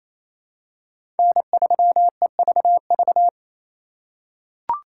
Morse code, heard as a keyed tone, about 28 wpm, 700 Hz. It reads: N3EVV
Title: USA Call Signs - 5 Characters - 2x Character Spacing - 28wpm